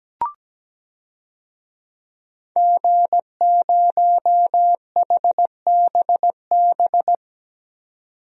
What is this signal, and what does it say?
Morse code, heard as a keyed tone, about 17 wpm, 700 Hz. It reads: G0HBB
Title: DXCC-100 Call Signs - 5 Characters - 17wpm